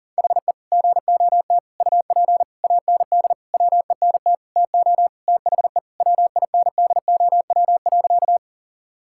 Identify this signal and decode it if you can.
Morse code, heard as a keyed tone, about 40 wpm, 700 Hz. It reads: HE GOT UP AND WENT TO THE WINDOW.
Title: Sentences from Top 1600 Words 40wpm